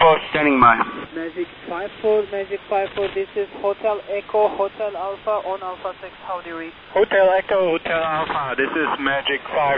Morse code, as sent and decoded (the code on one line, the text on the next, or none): none